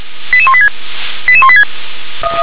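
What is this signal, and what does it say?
Five-tone selective call (ZVEI), heard as quick runs of five tones, tones of about 70 ms. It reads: 88186 88186
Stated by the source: ZVEI five-tone selective call received on an SDR.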